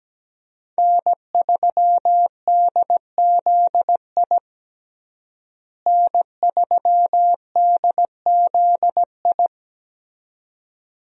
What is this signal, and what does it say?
Morse code, heard as a keyed tone, about 17 wpm, 700 Hz. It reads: N3DZI N3DZI
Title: USA Call Signs - 5 Characters - Repeated 2x - 17wpm